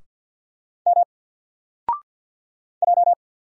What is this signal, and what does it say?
Morse code, heard as a keyed tone, about 50 wpm, 700 Hz. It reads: M J
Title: Single Letter-Number 50wpm